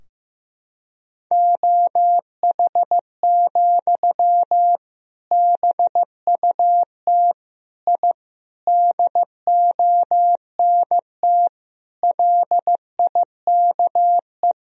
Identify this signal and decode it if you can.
Morse code, heard as a keyed tone, about 15 wpm, 700 Hz. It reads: OH, BUT I DONT LIKE
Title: Sentences from Top 1000 Words - Review Part 3 - 15wpm